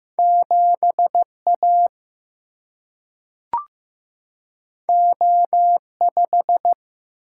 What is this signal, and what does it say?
Morse code, heard as a keyed tone, about 15 wpm, 700 Hz. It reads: 7 A O 5
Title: Single-Letter Single-Number 15wpm